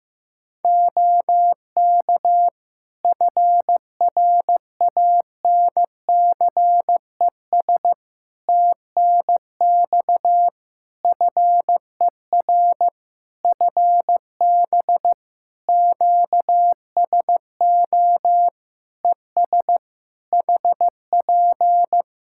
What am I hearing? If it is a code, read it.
Morse code, heard as a keyed tone, about 15 wpm, 700 Hz. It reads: OK FRANCES TNX FER FB QSO ES HP
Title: Practice QSO - Exchange 4 - Ending - 15wpm